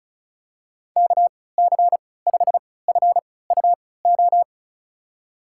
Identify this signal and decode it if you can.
Morse code, heard as a keyed tone, about 34 wpm, spaced as for 20 wpm, 700 Hz. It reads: KC5FUO
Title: USA Call Signs - 6 Characters - 3x Character Spacing - 35wpm